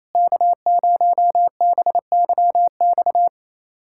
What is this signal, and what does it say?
Morse code, heard as a keyed tone, about 28 wpm, 700 Hz. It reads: K0BYX
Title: USA Callsign DE Callsign - 5 Characters - 28wpm